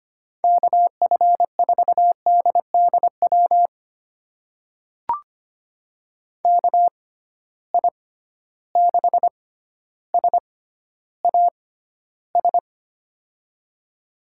Morse code, heard as a keyed tone, about 25 wpm, spaced as for 13 wpm, 700 Hz. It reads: KF4DDW KI6SAS
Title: USA Call Signs - 6 Characters - 6x Character Spacing - 25wpm